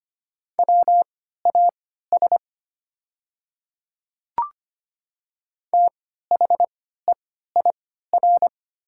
Morse code, heard as a keyed tone, about 25 wpm, spaced as for 14 wpm, 690 Hz. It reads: WAS THEIR
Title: 3x Character Spacing - Top 100 Words - 25wpm